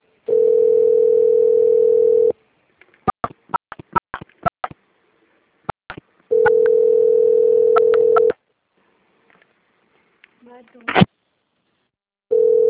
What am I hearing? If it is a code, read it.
Ringback tone (North America), 440 + 480 Hz, with speech.
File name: rg-1600-unknown- - - .7000